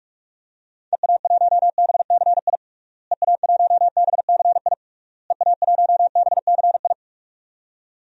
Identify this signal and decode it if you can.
Morse code, heard as a keyed tone, about 43 wpm, 700 Hz. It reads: EA1BCI EA1BCI EA1BCI
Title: DXCC-100 Call Signs - 6 Characters - Repeated 3x - 45wpm